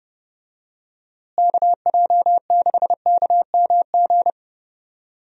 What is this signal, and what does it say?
Morse code, heard as a keyed tone, about 32 wpm, 700 Hz. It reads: KJ6KMG
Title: USA Call Signs - 6 Characters - Repeated 2x - 30wpm